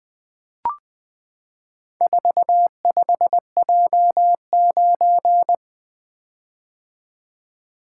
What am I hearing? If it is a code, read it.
Morse code, heard as a keyed tone, about 20 wpm, 700 Hz. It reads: 45J9
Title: Four Letter-Number Combinations - 20wpm